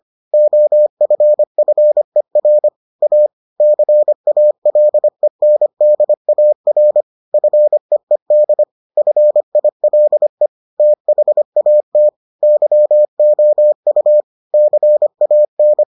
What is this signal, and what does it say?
Morse code, heard as a keyed tone, about 25 wpm, 600 Hz. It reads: OFFER A CALENDAR FEED FILE THAT YOU CAN